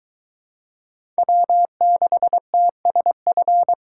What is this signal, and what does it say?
Morse code, heard as a keyed tone, about 23 wpm, 700 Hz. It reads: W6TSF